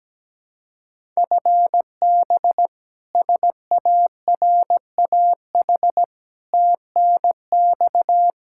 Morse code, heard as a keyed tone, about 17 wpm, 700 Hz. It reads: FB SARAH TNX
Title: K1USN SST QSO Practice 17wpm